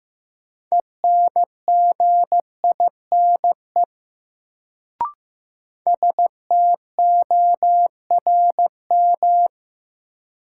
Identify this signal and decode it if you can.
Morse code, heard as a keyed tone, about 15 wpm, 700 Hz. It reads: ENGINE STORM
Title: Sentences from Top 1500 Words - New Word Drill Alt - 15wpm